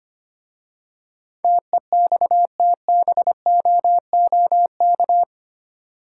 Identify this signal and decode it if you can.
Morse code, heard as a keyed tone, about 25 wpm, 700 Hz. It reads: TEXTBOOK
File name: practice / 25wpm / Easy Words with Hints - 25wpm